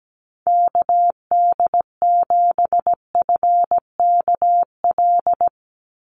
Morse code, heard as a keyed tone, about 17 wpm, 700 Hz. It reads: KD7FKL
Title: USA Call Signs 17wpm